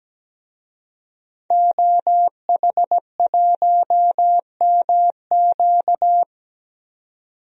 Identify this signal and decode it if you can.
Morse code, heard as a keyed tone, about 17 wpm, 700 Hz. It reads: OH1MQ
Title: DXCC-100 Call Signs - 5 Characters - 17wpm